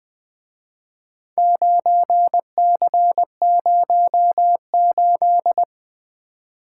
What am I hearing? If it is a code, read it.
Morse code, heard as a keyed tone, about 24 wpm, 700 Hz. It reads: 9C08
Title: Speed-Racing - Four Letter-Number Combinations - 20wpm